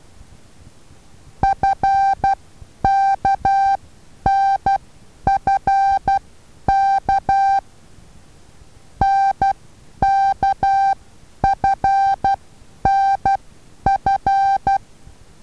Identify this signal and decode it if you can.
Morse code, heard as a keyed tone, about 12 wpm, spaced as for 9 wpm, 780 Hz. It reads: FKNFK NKFNF
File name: Taviro HA6VA B